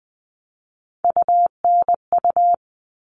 Morse code, heard as a keyed tone, about 20 wpm, 700 Hz. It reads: UNU
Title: All Cut Numbers - 3-Digit 20wpm